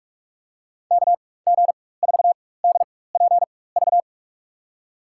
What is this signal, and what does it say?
Morse code, heard as a keyed tone, about 46 wpm, spaced as for 21 wpm, 700 Hz. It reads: KG4DPV
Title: USA Call Signs - 6 Characters - 4x Character Spacing - 45wpm